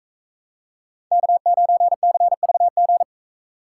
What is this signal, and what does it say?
Morse code, heard as a keyed tone, about 44 wpm, 700 Hz. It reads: K9CVG